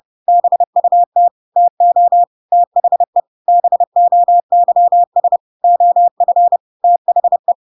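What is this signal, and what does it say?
Morse code, heard as a keyed tone, about 30 wpm, 700 Hz. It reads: BUT TO THE BOYS OF THE
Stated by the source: Chapter 2 of The Adventures of Pinocchio read as a book in Morse code (CW) at 30 wpm, by ik1hge.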